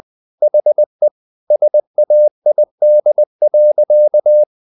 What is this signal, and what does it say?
Morse code, heard as a keyed tone, about 20 wpm, 600 Hz. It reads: HE SAID.
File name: the old man and the sea part 2 20wpm